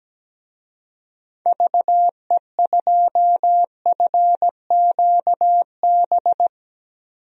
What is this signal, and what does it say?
Morse code, heard as a keyed tone, about 17 wpm, 700 Hz. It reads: VE2FQB
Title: DXCC-100 Call Signs - 6 Characters - Repeated 2x - 17wpm